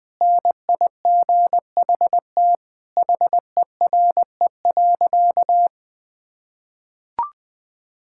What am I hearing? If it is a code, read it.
Morse code, heard as a keyed tone, about 20 wpm, 700 Hz. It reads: NIGHT HERE.
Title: Sentences from Top 1000 Words - Review Part 5 - 20wpm